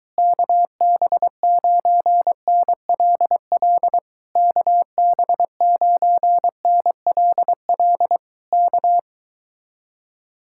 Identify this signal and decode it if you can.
Morse code, heard as a keyed tone, about 22 wpm, 700 Hz. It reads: KB9NLL KB9NLL K